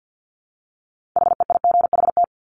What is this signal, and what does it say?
Morse code, heard as a keyed tone, about 50 wpm, 700 Hz. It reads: HEIGHT
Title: Multi-letter Phonemes 50wpm